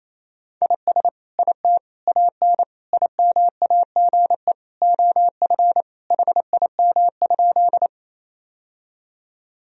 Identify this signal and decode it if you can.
Morse code, heard as a keyed tone, about 28 wpm, 700 Hz. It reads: IS IT AN IMAGE OF HIM?
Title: Sentences from Top 1000 Words - Review Part 4 - 28wpm